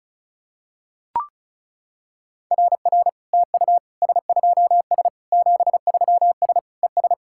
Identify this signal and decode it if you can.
Morse code, heard as a keyed tone, about 34 wpm, 690 Hz. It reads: RR TU S2S 73S ES
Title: SOTA QSO Practice 35wpm